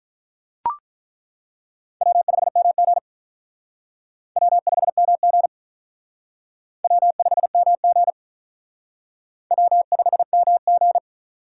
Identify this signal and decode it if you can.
Morse code, heard as a keyed tone, about 42 wpm, 700 Hz. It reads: W5MG W5MG W5MG W5MG
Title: Speed Racing - USA 1x2 Call Signs 35wpm